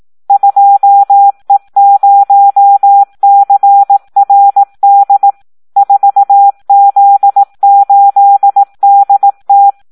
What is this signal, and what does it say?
Morse code, heard as a keyed tone, about 18 wpm, 800 Hz. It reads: 2E0CRD 4Z8DT
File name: Audio practica CW / SD 1 18